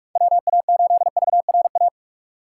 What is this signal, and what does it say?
Morse code, heard as a keyed tone, about 42 wpm, 700 Hz. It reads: WA8VRA
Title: USA Call Signs - 6 Characters - 4x Character Spacing - 45wpm